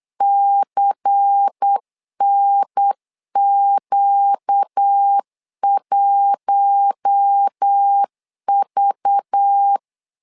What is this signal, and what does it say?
Morse code, heard as a keyed tone, about 8 wpm, 800 Hz. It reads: CNQ1V
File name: LEZIONI / 5 LBVC / 5 LBVC 8wpm 2935